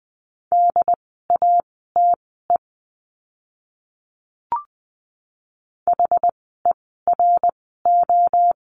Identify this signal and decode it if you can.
Morse code, heard as a keyed tone, about 20 wpm, spaced as for 14 wpm, 700 Hz. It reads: DATE HERO
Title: 068 Lesson 12f - Letter d - Words 20wpm